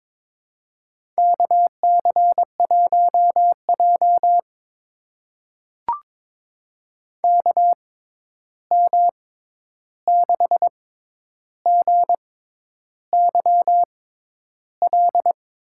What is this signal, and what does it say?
Morse code, heard as a keyed tone, about 22 wpm, spaced as for 10 wpm, 700 Hz. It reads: KC1J KM6GYL
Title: USA Call Signs - 6x Character Spacing - 22wpm